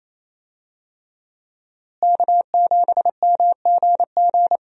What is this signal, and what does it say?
Morse code, heard as a keyed tone, about 28 wpm, 700 Hz. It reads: K7MGG
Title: USA Call Signs - 5 Characters - Repeated 3x - 28wpm